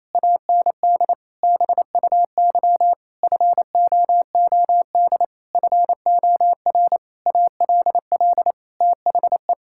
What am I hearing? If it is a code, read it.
Morse code, heard as a keyed tone, about 28 wpm, 700 Hz. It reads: AND BUY FOOD FOR ALL THE
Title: Sentences from Top 1600 Words 28wpm